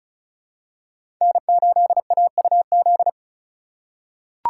Morse code, heard as a keyed tone, about 34 wpm, 700 Hz. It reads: N8AUZ
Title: USA Call Signs - 5 Characters - 4x Character Spacing - 35wpm